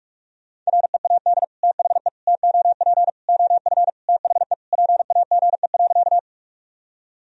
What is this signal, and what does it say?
Morse code, heard as a keyed tone, about 46 wpm, 700 Hz. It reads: READ THE TOP OF THE PAGE.
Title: Sentences from Top 1000 Words - Review Part 2 - 45wpm